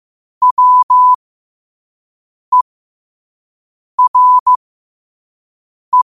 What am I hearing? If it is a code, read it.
Morse code, heard as a keyed tone, about 15 wpm, spaced as for 5 wpm, 1000 Hz. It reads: WERE
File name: qotdfiles / 5wpm